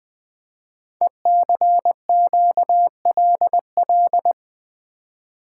Morse code, heard as a keyed tone, about 21 wpm, 700 Hz. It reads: ECQLL
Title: Code Groups - 5 Letters - 20wpm